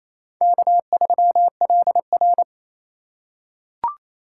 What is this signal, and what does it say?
Morse code, heard as a keyed tone, about 28 wpm, 700 Hz. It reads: K3LR
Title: K1USN SST QSO Practice - 28wpm